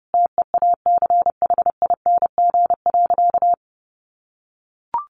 Morse code, heard as a keyed tone, about 30 wpm, 700 Hz. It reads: TEACHING.
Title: Sentences from Top 1100 Words 30wpm